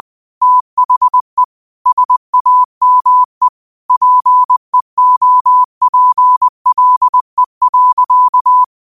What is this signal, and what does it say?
Morse code, heard as a keyed tone, about 20 wpm, 1000 Hz. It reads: THE SAME PEOPLE.
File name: qotdfiles / 20wpm